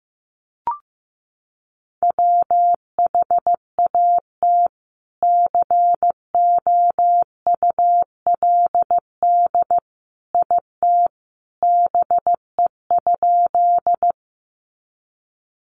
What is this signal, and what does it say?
Morse code, heard as a keyed tone, about 14 wpm, 700 Hz. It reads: WHAT COULD IT BE?
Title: Sentences from Top 100 Words 15wpm